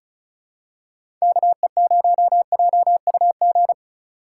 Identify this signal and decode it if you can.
Morse code, heard as a keyed tone, about 36 wpm, 700 Hz. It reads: KE0JUG
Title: USA Call Signs - 6 Characters - 2x Character Spacing - 35wpm